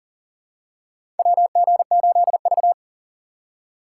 Morse code, heard as a keyed tone, about 39 wpm, 700 Hz. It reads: WG8V
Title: USA Call Signs - 4 Characters - 2x Character Spacing - 40wpm